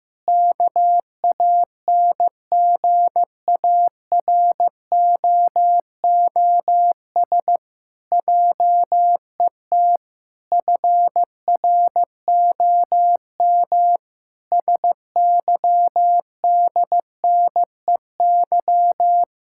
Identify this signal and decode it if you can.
Morse code, heard as a keyed tone, about 16 wpm, 700 Hz. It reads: KANGAROOS JET FROM SYDNEY
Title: Pangram Challenge v2 - 15wpm